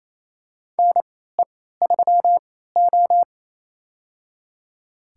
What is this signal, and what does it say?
Morse code, heard as a keyed tone, about 28 wpm, spaced as for 16 wpm, 700 Hz. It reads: NE3O
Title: USA Call Signs - 4 Characters - 3x Character Spacing - 28wpm